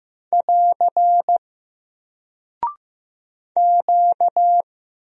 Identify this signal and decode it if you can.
Morse code, heard as a keyed tone, about 15 wpm, 700 Hz. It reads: <AR> Q
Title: Single Character (Letter:Number:Punctuation:Prosign) Alt 15wpm